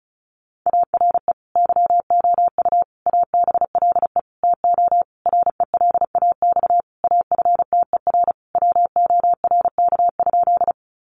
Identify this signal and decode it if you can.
Morse code, heard as a keyed tone, about 34 wpm, 700 Hz. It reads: ARE YOU ABLE TO RELAX AFTER WORK?
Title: Sentences from Top 1400 Words 35wpm